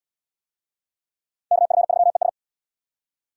Morse code, heard as a keyed tone, about 78 wpm, 690 Hz. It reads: KF3ES